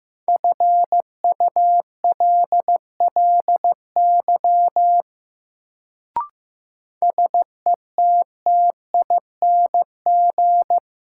Morse code, heard as a keyed tone, about 15 wpm, 700 Hz. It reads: FULLY SETTING
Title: Sentences from Top 1300 Words - New Word Drill Alt - 15wpm